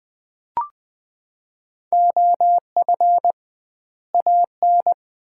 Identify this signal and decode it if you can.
Morse code, heard as a keyed tone, about 20 wpm, 700 Hz. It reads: OF AN
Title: 2x Word-Spacing - Sets of 2 Words - 20wpm